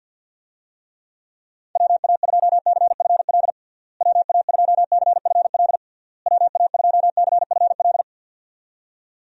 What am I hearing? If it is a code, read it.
Morse code, heard as a keyed tone, about 50 wpm, 700 Hz. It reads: WA2CFL WA2CFL WA2CFL
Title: USA Call Signs - 6 Characters - Repeated 3x - 50wpm